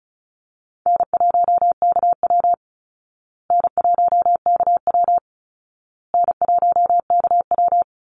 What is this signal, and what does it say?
Morse code, heard as a keyed tone, about 34 wpm, 700 Hz. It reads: N1KW N1KW N1KW
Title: Call Signs Repeated 2x - 35wpm